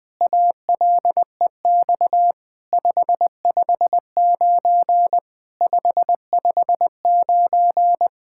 Morse code, heard as a keyed tone, about 20 wpm, 700 Hz. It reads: ALEX 559 559
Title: SOTA QSO Practice 20wpm